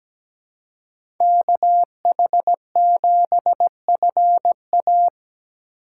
Morse code, heard as a keyed tone, about 17 wpm, 700 Hz. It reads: KH7FA
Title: USA Call Signs - 5 Characters - 6x Character Spacing - 17wpm